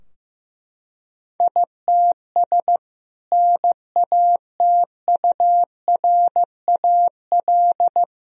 Morse code, heard as a keyed tone, about 15 wpm, 700 Hz. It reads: ITS NATURAL
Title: Sentences from Top 1000 Words - Review Part 4 - 15wpm